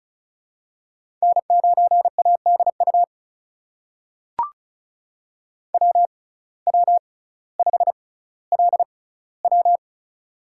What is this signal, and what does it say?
Morse code, heard as a keyed tone, about 36 wpm, spaced as for 18 wpm, 700 Hz. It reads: N9ADU WW5LW
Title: USA Call Signs - 5 Characters - 6x Character Spacing - 35wpm